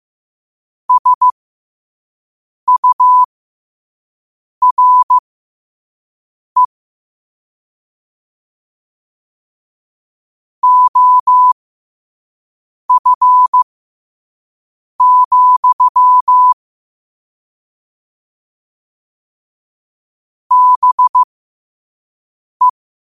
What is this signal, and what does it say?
Morse code, heard as a keyed tone, about 15 wpm, spaced as for 5 wpm, 1000 Hz. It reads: SURE OF, BE
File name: qotdfiles / 5wpm